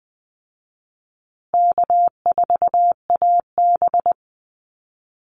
Morse code, heard as a keyed tone, about 19 wpm, 700 Hz. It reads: K4AB
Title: Call Signs Repeated 2x - 20wpm